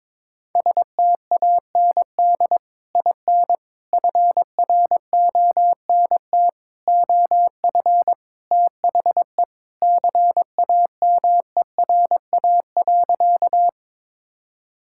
Morse code, heard as a keyed tone, about 22 wpm, 700 Hz. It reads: STAND IN FRONT OF THE CAMERA.
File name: practice / 22wpm / Sentences with 3-Word Phrases - 22wpm